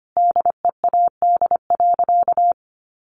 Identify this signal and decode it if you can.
Morse code, heard as a keyed tone, about 25 wpm, 700 Hz. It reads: DEAD.
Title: A Sound Of Thunder 25wpm